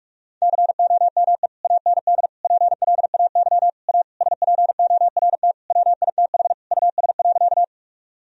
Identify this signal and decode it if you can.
Morse code, heard as a keyed tone, about 46 wpm, 700 Hz. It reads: COME AND PLAY A SPORT WITH US.
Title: Sentences from Top 1000 Words - Review Part 4 - 45wpm